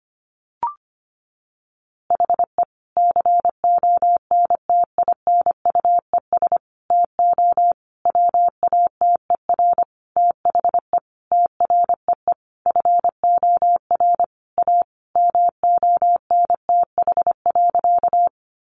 Morse code, heard as a keyed tone, about 25 wpm, 700 Hz. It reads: HE CONTINUES TO WATER THE TREE FOR A MONTH.
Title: Easy Phrases 25wpm